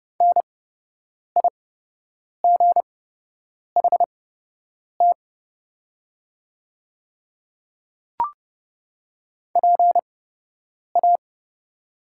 Morse code, heard as a keyed tone, about 30 wpm, spaced as for 8 wpm, 700 Hz. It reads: NIGHT PA
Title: 8x Character Spacing - Top 300 Words - 30wpm